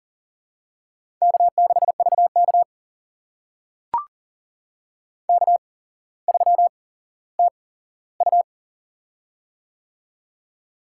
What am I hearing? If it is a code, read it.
Morse code, heard as a keyed tone, about 40 wpm, spaced as for 17 wpm, 700 Hz. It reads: K6VK K3TU
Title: USA Call Signs - 4 Characters - 8x Character Spacing - 40wpm